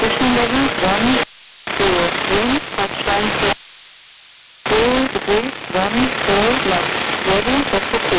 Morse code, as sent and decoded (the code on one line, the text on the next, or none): none